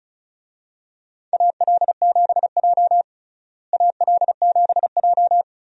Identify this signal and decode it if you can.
Morse code, heard as a keyed tone, about 36 wpm, 700 Hz. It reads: AL7J AL7J
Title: USA Call Signs - 4 Characters - Repeated 2x - 35wpm